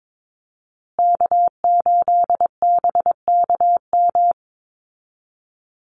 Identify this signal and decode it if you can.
Morse code, heard as a keyed tone, about 22 wpm, 700 Hz. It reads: K8BKM
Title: Call Signs Repeated 2x - 22wpm